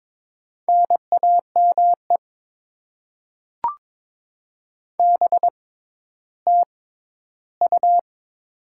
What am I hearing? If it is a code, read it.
Morse code, heard as a keyed tone, about 22 wpm, spaced as for 12 wpm, 690 Hz. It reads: NAME BTU
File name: Top 20 QSO Elements - 6x Character Spacing - 22wpm